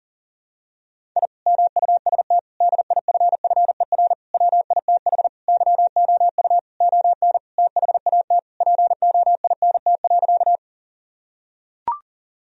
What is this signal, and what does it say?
Morse code, heard as a keyed tone, about 40 wpm, 700 Hz. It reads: I MUST DIFFER WITH YOU ON THAT POINT.
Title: Sentences from Top 1000 Words - Review Part 1 - 40wpm